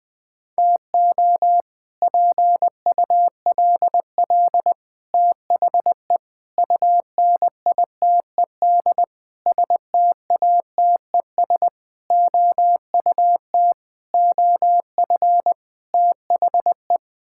Morse code, heard as a keyed tone, about 20 wpm, 700 Hz. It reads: TO PULL THE UNITED STATES OUT OF THE
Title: Sets of 8 Words - 20wpm